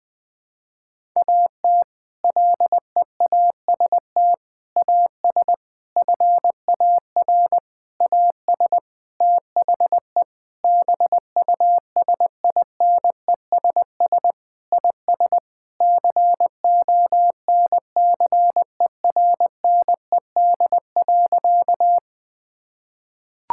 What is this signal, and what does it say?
Morse code, heard as a keyed tone, about 20 wpm, 700 Hz. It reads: AT LEAST AS FAR AS THE BUSINESS IS CONCERNED.
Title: Sentences from Top 1000 Words - Review Part 6 - 20wpm